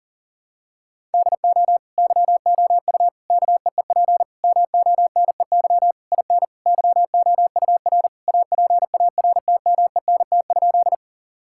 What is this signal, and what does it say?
Morse code, heard as a keyed tone, about 37 wpm, 700 Hz. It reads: DO YOU KEEP MONEY IN YOUR APARTMENT?
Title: Sentences from Top 1000 Words - Review Part 6 - 40wpm